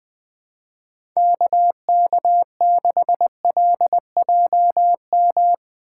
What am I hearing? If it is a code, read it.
Morse code, heard as a keyed tone, about 20 wpm, 700 Hz. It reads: KK6LJM
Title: Speed Racing - USA 2x3 Call Signs 17wpm